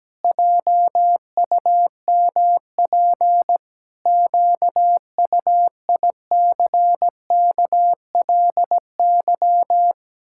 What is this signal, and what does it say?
Morse code, heard as a keyed tone, about 17 wpm, 700 Hz. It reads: JUMP QUICKLY
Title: Pangram Challenge v2 - 17wpm